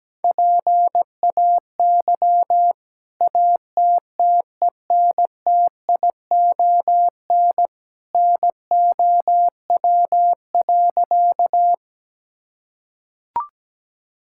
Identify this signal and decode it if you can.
Morse code, heard as a keyed tone, about 17 wpm, 700 Hz. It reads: PAY ATTENTION NOW.
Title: Sentences from Top 1000 Words - Review Part 6 - 17wpm